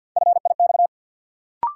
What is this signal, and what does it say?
Morse code, heard as a keyed tone, about 52 wpm, 700 Hz. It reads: FIX
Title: Sentences from Top 1400 Words - New Word Drill Alt - 50wpm